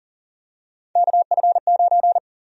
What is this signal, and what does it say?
Morse code, heard as a keyed tone, about 39 wpm, 700 Hz. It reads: KF9
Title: USA Call Sign Prefixes - 40wpm